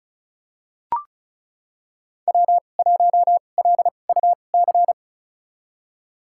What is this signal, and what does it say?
Morse code, heard as a keyed tone, about 36 wpm, spaced as for 25 wpm, 700 Hz. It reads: W1LUC
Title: USA Call Signs - 2x Character Spacing - 35wpm